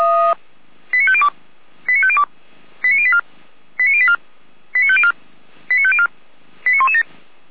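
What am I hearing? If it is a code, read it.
Five-tone selective call (ZVEI), heard as quick runs of five tones, tones of about 70 ms. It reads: 89502 89502 89084 89084 89504 89504 89117
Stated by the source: Five-tone ZVEI selective call received on an SDR.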